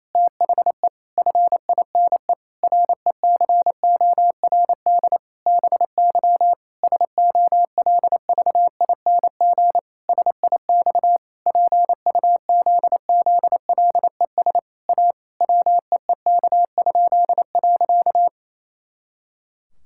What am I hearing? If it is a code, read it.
Morse code, heard as a keyed tone, about 28 wpm, 700 Hz. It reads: THE FINE RECORD BY SOLVING SIX PUZZLES A WEEK?.
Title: Pangram Challenge v2 - 28wpm